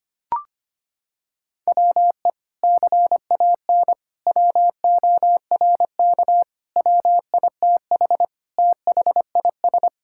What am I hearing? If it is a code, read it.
Morse code, heard as a keyed tone, about 25 wpm, 700 Hz. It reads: WE CAN WORK WITH THIS
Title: Sentences from Top 1000 Words - Review Part 5 - 25wpm